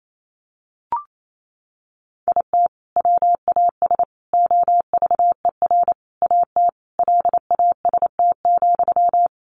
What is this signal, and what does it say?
Morse code, heard as a keyed tone, about 28 wpm, 700 Hz. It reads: IT WAS OVER AT LAST,
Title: Sentences from Top 600 Words 28wpm